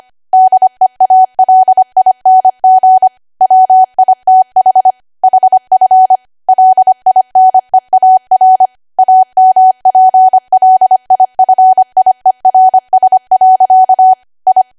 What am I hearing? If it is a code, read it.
Morse code, heard as a keyed tone, about 25 wpm, 750 Hz. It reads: DEALING WITH HF LINEAR AMPLIFIERS. I
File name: Archive / 25 WPM / 25WPM